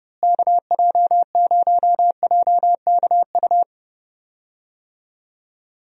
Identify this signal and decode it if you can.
Morse code, heard as a keyed tone, about 30 wpm, 700 Hz. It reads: KJ0JKU
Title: USA Call Signs - 6 Characters - Repeated 2x - 30wpm